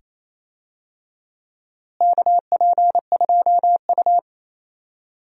Morse code, heard as a keyed tone, about 28 wpm, 700 Hz. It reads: KP2U
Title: DXCC-100 Call Signs - 4 Characters - Repeated 2x - 28wpm